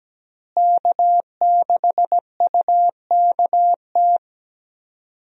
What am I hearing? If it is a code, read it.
Morse code, heard as a keyed tone, about 17 wpm, 700 Hz. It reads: K6UKT
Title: USA Call Signs - 5 Characters - Repeated 3x - 17wpm